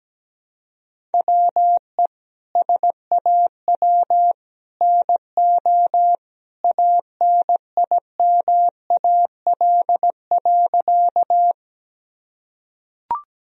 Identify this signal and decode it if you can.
Morse code, heard as a keyed tone, about 17 wpm, 700 Hz. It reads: WE SAW NO ANIMAL.
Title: Sentences from Top 1000 Words - Review Part 1 - 17wpm